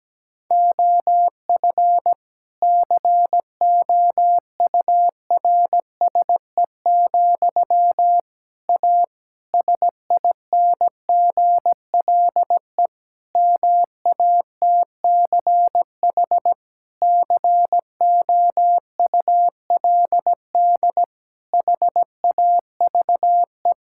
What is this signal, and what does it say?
Morse code, heard as a keyed tone, about 17 wpm, 700 Hz. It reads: OF COURSE, A SINGLE MATCH COULD HAVE
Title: Sentences from Top 1600 Words 17wpm